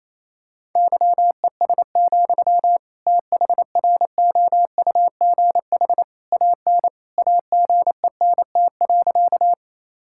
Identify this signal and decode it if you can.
Morse code, heard as a keyed tone, about 28 wpm, 700 Hz. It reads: YES, THROUGH AN AGENT.
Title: Sentences from Top 1600 Words 28wpm